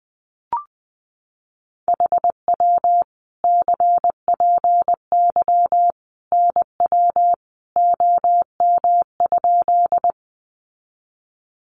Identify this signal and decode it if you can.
Morse code, heard as a keyed tone, about 20 wpm, 700 Hz. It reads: HW CPY NW OM?
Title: Common QSO Phrases 20wpm